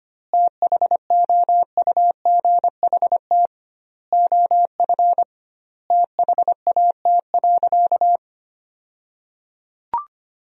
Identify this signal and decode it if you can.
Morse code, heard as a keyed tone, about 25 wpm, 700 Hz. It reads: THOUGHT OF THAT.
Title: 2x Word-Spacing - Sentences from Top 300 Words - 25wpm